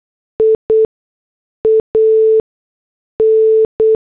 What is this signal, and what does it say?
Morse code, heard as a keyed tone, about 8 wpm, spaced as for 6 wpm, 440 Hz. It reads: IAN
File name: morse / headlines